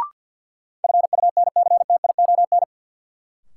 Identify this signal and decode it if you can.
Morse code, heard as a keyed tone, about 53 wpm, 700 Hz. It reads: FUNCTION
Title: Sentences from Top 1100 Words - New Word Drill Alt - 50wpm